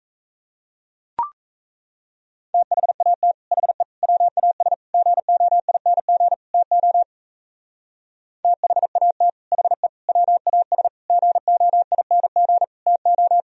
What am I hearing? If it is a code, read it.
Morse code, heard as a keyed tone, about 33 wpm, 700 Hz. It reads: THAT HE WAS GOING TO THAT HE WAS GOING TO
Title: Speed-Racing - Sets of 5 Words 28wpm